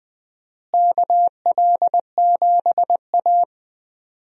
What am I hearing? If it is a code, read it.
Morse code, heard as a keyed tone, about 20 wpm, 700 Hz. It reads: KL7A
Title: DXCC-100 Call Signs - 4 Characters - Repeated 3x - 20wpm